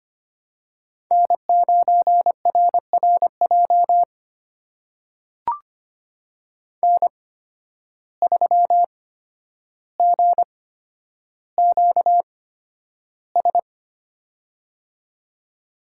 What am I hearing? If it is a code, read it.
Morse code, heard as a keyed tone, about 25 wpm, spaced as for 11 wpm, 700 Hz. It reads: N9RRJ N3GQS
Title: USA Call Signs - 5 Characters - 8x Character Spacing - 25wpm